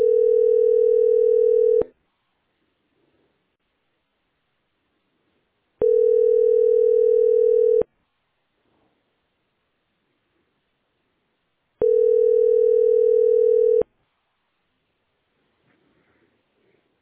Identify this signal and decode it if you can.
Ringback tone (North America), 440 + 480 Hz.